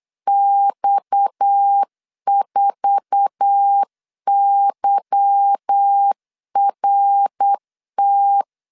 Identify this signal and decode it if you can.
Morse code, heard as a keyed tone, about 8 wpm, 800 Hz. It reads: X4YRT
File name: LEZIONI / 10 SIGNS / 10B INT 8wpm 3534